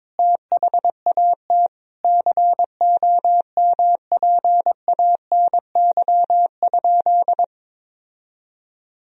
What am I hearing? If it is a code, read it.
Morse code, heard as a keyed tone, about 21 wpm, 700 Hz. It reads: THAT COMPANY?
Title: Sentences from Top 1000 Words - Review Part 3 - 22wpm